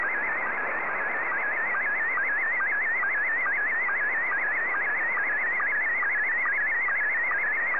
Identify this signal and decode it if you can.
SSTV (slow-scan television), Scottie 1.
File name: sstv / S1-VK2TU